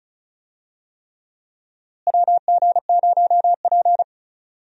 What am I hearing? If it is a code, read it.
Morse code, heard as a keyed tone, about 36 wpm, 700 Hz. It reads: WG0P
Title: USA Call Signs - 4 Characters - Repeated 3x - 35wpm